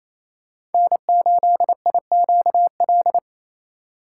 Morse code, heard as a keyed tone, about 28 wpm, 700 Hz. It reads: N8IQL
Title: USA Call Signs - 4x Character Spacing - 28wpm